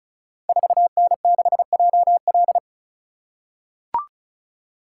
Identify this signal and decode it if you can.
Morse code, heard as a keyed tone, about 36 wpm, 690 Hz. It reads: 4N6JL
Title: Code Groups - 5 Letter:Numbers - 35wpm